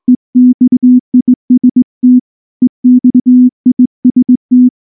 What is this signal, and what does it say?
Morse code, heard as a keyed tone, about 18 wpm, 250 Hz. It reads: EXIST EXIST
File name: morse / morse-training-20WPM-5